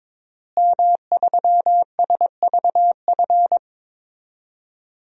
Morse code, heard as a keyed tone, about 22 wpm, 700 Hz. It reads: M3SVF
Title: DXCC-100 Call Signs - 5 Characters - Repeated 2x - 22wpm